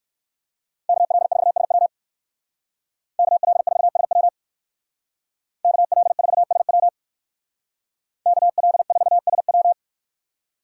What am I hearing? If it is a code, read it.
Morse code, heard as a keyed tone, about 56 wpm, 700 Hz. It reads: KL4SW KL4SW KL4SW KL4SW
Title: Speed Racing - USA 2x2 Call Signs 45wpm